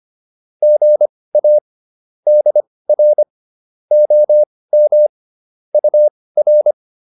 Morse code, heard as a keyed tone, about 25 wpm, spaced as for 18 wpm, 600 Hz. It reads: GA DR OM UR